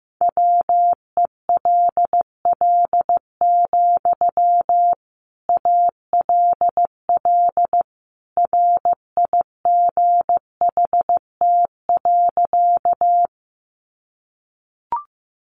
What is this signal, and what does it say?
Morse code, heard as a keyed tone, about 15 wpm, 700 Hz. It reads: WELL, ALL RIGHT.
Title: Sentences from Top 200 Words 15wpm